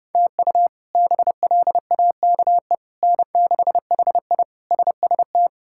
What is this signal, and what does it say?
Morse code, heard as a keyed tone, about 32 wpm, 700 Hz. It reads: TU BLAKE N6HI SST
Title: K1USN SST QSO Practice - 30wpm